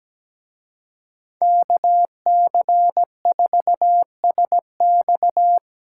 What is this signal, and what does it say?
Morse code, heard as a keyed tone, about 20 wpm, 700 Hz. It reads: KC4SX
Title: Speed Racing - USA Call Signs 17wpm